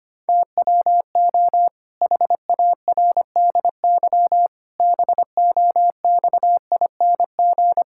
Morse code, heard as a keyed tone, about 25 wpm, 700 Hz. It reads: TWO HARDY BOXING
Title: Pangram Challenge v2 - 25wpm